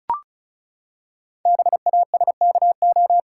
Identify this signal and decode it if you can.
Morse code, heard as a keyed tone, about 36 wpm, 690 Hz. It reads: BASKO
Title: Code Groups - 5 Letters - 35wpm